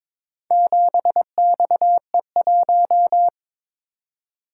Morse code, heard as a keyed tone, about 22 wpm, 700 Hz. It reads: 7XE1
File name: practice / 22wpm / Four Letter-Number Combinations - 22wpm